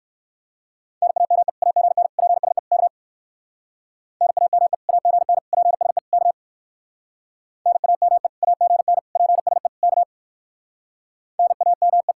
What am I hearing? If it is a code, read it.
Morse code, heard as a keyed tone, about 59 wpm, 700 Hz. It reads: NAME AGN PSE K NAME AGN PSE K NAME AGN PSE K NAME